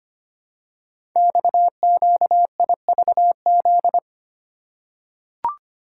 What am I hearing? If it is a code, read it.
Morse code, heard as a keyed tone, about 25 wpm, 700 Hz. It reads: XQIVZ
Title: Code Groups - 5 Letters - 25wpm